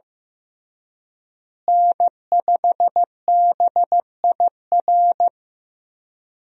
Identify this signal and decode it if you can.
Morse code, heard as a keyed tone, about 16 wpm, 700 Hz. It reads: N5BIR
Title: USA Call Signs - 5 Characters - Repeated 3x - 15wpm